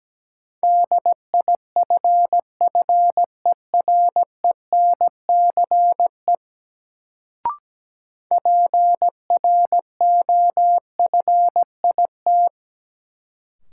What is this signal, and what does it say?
Morse code, heard as a keyed tone, about 17 wpm, 700 Hz. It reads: DIFFERENCE PROFIT
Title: Sentences from Top 800 Words - New Word Drill - 17wpm